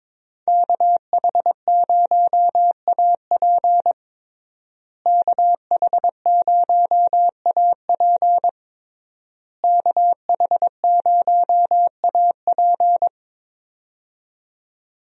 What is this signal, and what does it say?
Morse code, heard as a keyed tone, about 21 wpm, 700 Hz. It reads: KH0AP KH0AP KH0AP
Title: USA Call Signs - 5 Characters - Repeated 3x - 22wpm